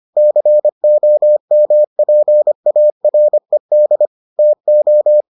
Morse code, heard as a keyed tone, about 25 wpm, 600 Hz. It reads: COMPARED TO